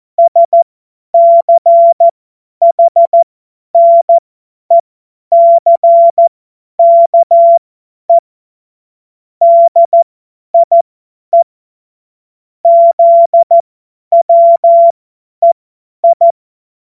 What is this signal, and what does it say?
Morse code, heard as a keyed tone, about 14 wpm, spaced as for 10 wpm, 680 Hz. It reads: SCHNECKE DIE ZWEI
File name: Morsekurs / Witze / Witz 16. Dezember 2025 50 F70